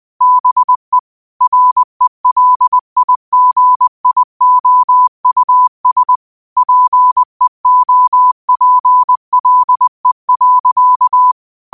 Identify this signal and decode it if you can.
Morse code, heard as a keyed tone, about 20 wpm, 1000 Hz. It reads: BE RELIGIOUS PEOPLE.
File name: qotdfiles / 20wpm